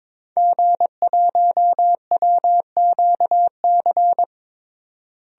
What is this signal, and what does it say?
Morse code, heard as a keyed tone, about 22 wpm, 700 Hz. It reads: G1WQC
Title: DXCC-100 Call Signs - 5 Characters - Repeated 2x - 22wpm